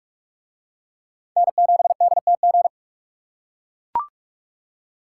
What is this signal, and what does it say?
Morse code, heard as a keyed tone, about 43 wpm, 700 Hz. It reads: N7DTG